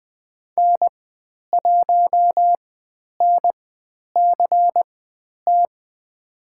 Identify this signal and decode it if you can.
Morse code, heard as a keyed tone, about 20 wpm, spaced as for 10 wpm, 700 Hz. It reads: N1NCT
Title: 095 Lesson 16g - Letter c - Call Signs 20wpm